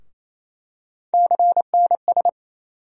Morse code, heard as a keyed tone, about 28 wpm, 700 Hz. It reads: CNS
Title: USA Call Sign Postfixes - 3 Letters 28wpm